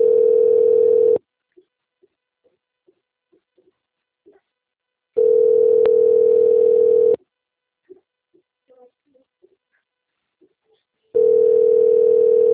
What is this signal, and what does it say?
Ringback tone (North America), 440 + 480 Hz.